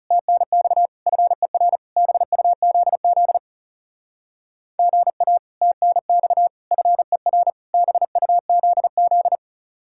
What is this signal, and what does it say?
Morse code, heard as a keyed tone, about 37 wpm, 700 Hz. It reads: TNX FER BUZZ GA TNX FER BUZZ
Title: Speed-Racing Common QSO Phrases - 30wpm